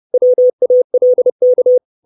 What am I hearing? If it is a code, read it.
Morse code, heard as a keyed tone, about 30 wpm, 500 Hz. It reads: WALK
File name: 632 Words 30WPM